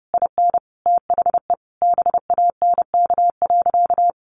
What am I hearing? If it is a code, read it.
Morse code, heard as a keyed tone, about 30 wpm, 700 Hz. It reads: IN THE BANK.
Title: Sentences from Top 1200 Words 30wpm